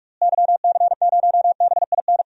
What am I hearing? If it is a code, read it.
Morse code, heard as a keyed tone, about 41 wpm, 700 Hz. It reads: YC0BIN
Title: DXCC-100 Call Signs - 6 Characters - Repeated 3x - 45wpm